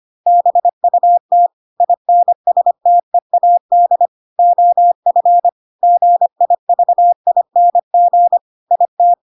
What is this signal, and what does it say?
Morse code, heard as a keyed tone, about 25 wpm, 700 Hz. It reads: BUT INSTEAD OF GIVING IT
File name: Pinocchio / Pinocchio ch3 25 25 ik1hge